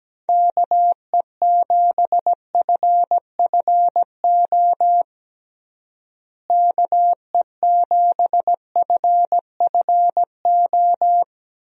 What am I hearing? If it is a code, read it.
Morse code, heard as a keyed tone, about 17 wpm, 700 Hz. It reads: KE7FFO KE7FFO
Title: USA Call Signs - 6 Characters - Repeated 3x - 17wpm